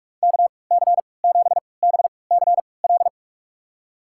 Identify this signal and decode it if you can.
Morse code, heard as a keyed tone, about 47 wpm, spaced as for 26 wpm, 700 Hz. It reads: KC7BCL
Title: USA Call Signs - 6 Characters - 3x Character Spacing - 45wpm